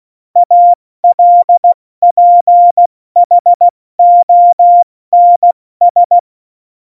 Morse code, heard as a keyed tone, about 16 wpm, 700 Hz. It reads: ALPHONS
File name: Morsekurs / Witze / Witz 30. Januar 2018, 70 F80